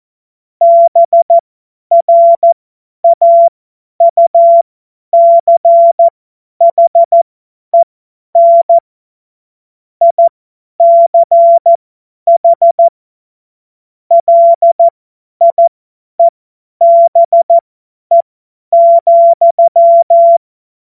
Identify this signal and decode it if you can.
Morse code, heard as a keyed tone, about 14 wpm, spaced as for 10 wpm, 680 Hz. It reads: BRAUCHEN ICH LIEBE,
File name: Morsekurs / Witze / Witz 19. April 2022 50 F70